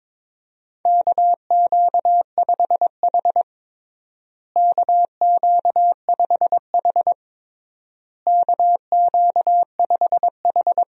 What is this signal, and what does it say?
Morse code, heard as a keyed tone, about 22 wpm, 700 Hz. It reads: KQ5H KQ5H KQ5H
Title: USA Call Signs - 4 Characters - Repeated 3x - 22wpm